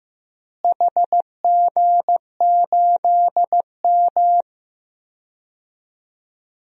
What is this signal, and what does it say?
Morse code, heard as a keyed tone, about 15 wpm, 700 Hz. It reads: HG8M
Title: DXCC-100 Call Signs - 4 Characters - Repeated 3x - 15wpm